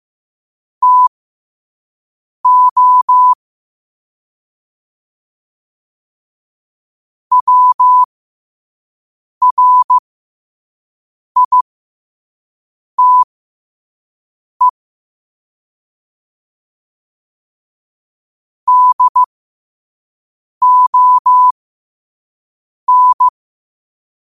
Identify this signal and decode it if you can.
Morse code, heard as a keyed tone, about 15 wpm, spaced as for 5 wpm, 1000 Hz. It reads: TO WRITE DON
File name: qotdfiles / 5wpm